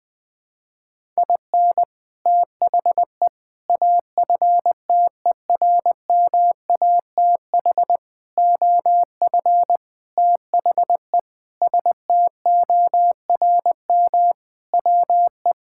Morse code, heard as a keyed tone, about 20 wpm, 700 Hz. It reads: IN THE AFTERMATH OF THE STORM WE
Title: Sentences with 5-Word Phrases - 20wpm